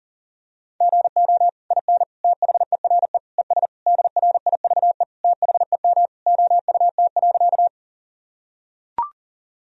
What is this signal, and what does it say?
Morse code, heard as a keyed tone, about 40 wpm, 700 Hz. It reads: GO IN THERE ES DRIVE THEM OUT.
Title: Sentences from Top 500 Words and Abbreviations 40wpm